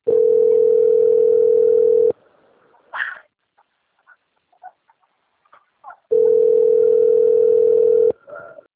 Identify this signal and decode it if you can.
Ringback tone (North America), 440 + 480 Hz.